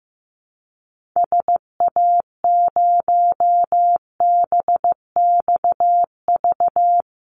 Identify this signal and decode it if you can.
Morse code, heard as a keyed tone, about 15 wpm, 700 Hz. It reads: SA0BXV
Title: Call Signs Repeated 3x - 15wpm